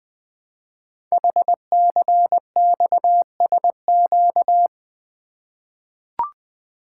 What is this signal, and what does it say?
Morse code, heard as a keyed tone, about 20 wpm, 700 Hz. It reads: HCXSQ
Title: Code Groups - 5 Letters - 20wpm